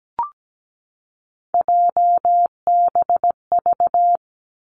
Morse code, heard as a keyed tone, about 17 wpm, 700 Hz. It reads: JBV
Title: Three Challenging Letter-Number Combinations 17wpm